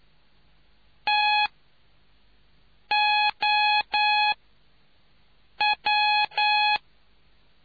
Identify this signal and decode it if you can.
Morse code, heard as a keyed tone, about 9 wpm, spaced as for 5 wpm, 840 Hz. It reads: TOW